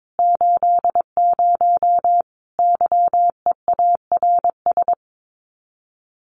Morse code, heard as a keyed tone, about 22 wpm, 700 Hz. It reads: 80 YEARS
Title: Common QSO Phrases 22wpm